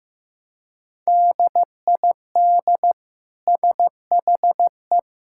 Morse code, heard as a keyed tone, about 15 wpm, 700 Hz. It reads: DID SHE
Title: Sentences from Top 1000 Words - Review Part 5 - 15wpm